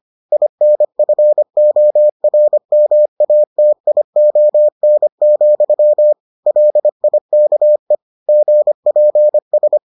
Morse code, heard as a keyed tone, about 25 wpm, 600 Hz. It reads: INFORMATION, LIKE GPS